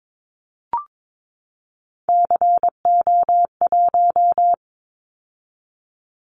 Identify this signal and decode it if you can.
Morse code, heard as a keyed tone, about 22 wpm, 700 Hz. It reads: CO1
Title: Callsign Prefixes for Top 100 DXCC - 22wpm